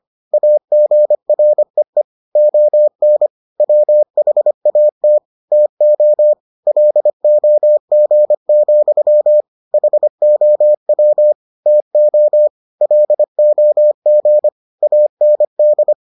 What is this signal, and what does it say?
Morse code, heard as a keyed tone, about 25 wpm, 600 Hz. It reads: AGREE ON WHAT TO LOG, HOW TO LOG AND